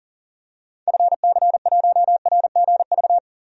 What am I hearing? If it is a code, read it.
Morse code, heard as a keyed tone, about 40 wpm, 700 Hz. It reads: FC1RGV
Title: DXCC-100 Call Signs - 6 Characters - 40wpm